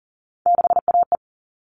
Morse code, heard as a keyed tone, about 40 wpm, 700 Hz. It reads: 6AE
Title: All Cut Numbers - 3-Digit 40wpm